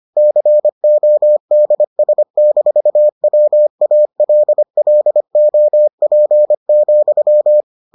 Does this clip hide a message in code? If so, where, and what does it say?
Yes: Morse code from 0.2 to 7.6 s, heard as a keyed tone, about 25 wpm, 600 Hz. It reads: CODS-WALLOP,